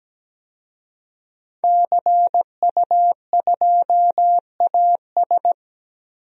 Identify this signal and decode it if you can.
Morse code, heard as a keyed tone, about 17 wpm, 700 Hz. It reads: CU2AS
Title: DXCC-100 Call Signs - 5 Characters - 17wpm